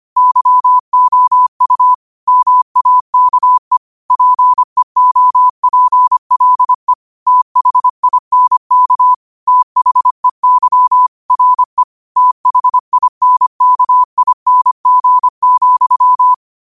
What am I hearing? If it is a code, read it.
Morse code, heard as a keyed tone, about 25 wpm, 1000 Hz. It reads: YOU MAKE PEOPLE THINK THEY RE THINKING,
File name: qotdfiles / 25wpm